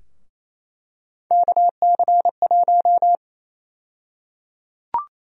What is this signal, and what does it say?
Morse code, heard as a keyed tone, about 28 wpm, 700 Hz. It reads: KC1
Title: USA Call Sign Prefixes - Repeated 2x - 28wpm